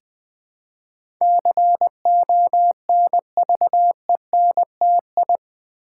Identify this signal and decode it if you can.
Morse code, heard as a keyed tone, about 21 wpm, 700 Hz. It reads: CONVENTI
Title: Build A Word - 4 Letters At A Time - 10-Letter Words - 20wpm